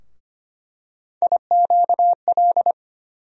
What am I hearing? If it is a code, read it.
Morse code, heard as a keyed tone, about 25 wpm, 700 Hz. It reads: IQL
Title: USA Call Sign Postfixes - 3 Letters 25wpm